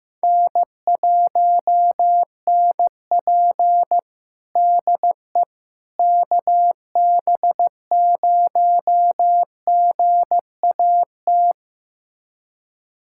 Morse code, heard as a keyed tone, about 15 wpm, 700 Hz. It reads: N1NP DE KB0GAT
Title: USA Callsign DE Callsign 15wpm